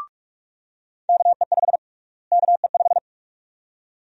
Morse code, heard as a keyed tone, about 45 wpm, 700 Hz. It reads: KE5 KE5
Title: USA Call Sign Prefixes - Repeated 2x - 45wpm